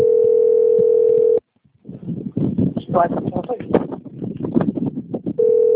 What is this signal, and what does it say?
Ringback tone (North America), 440 + 480 Hz, with speech.
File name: rg-1600-unknown- - - .26574